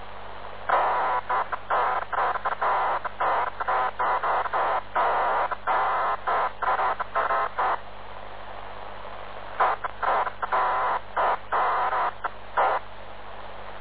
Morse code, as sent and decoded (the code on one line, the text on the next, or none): none